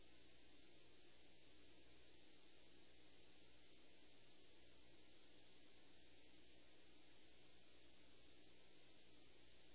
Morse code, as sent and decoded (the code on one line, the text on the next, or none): none